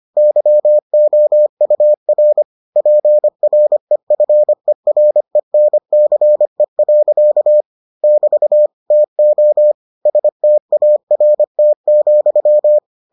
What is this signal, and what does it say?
Morse code, heard as a keyed tone, about 25 wpm, 600 Hz. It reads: YOUR PREFERENCE. <BT> TO START,